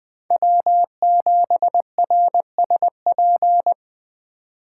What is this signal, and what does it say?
Morse code, heard as a keyed tone, about 21 wpm, 700 Hz. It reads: W7RSP